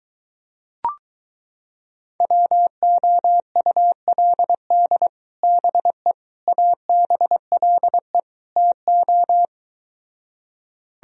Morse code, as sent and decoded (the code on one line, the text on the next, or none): .-- --- ..- .-.. -.. / -... / .- -... .-.. / - ---
WOULD BE ABLE TO